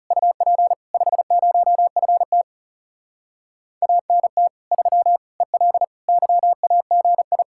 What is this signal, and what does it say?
Morse code, heard as a keyed tone, about 40 wpm, 700 Hz. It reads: UP 50FT ANT 3 EL YAGI